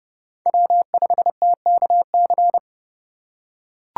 Morse code, heard as a keyed tone, about 30 wpm, 700 Hz. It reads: W5TKC
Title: USA Call Signs - 5 Characters - 2x Character Spacing - 30wpm